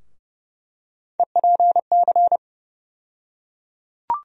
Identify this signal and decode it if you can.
Morse code, heard as a keyed tone, about 31 wpm, 700 Hz. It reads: EPC
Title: USA Call Sign Postfixes - 3 Letters 30wpm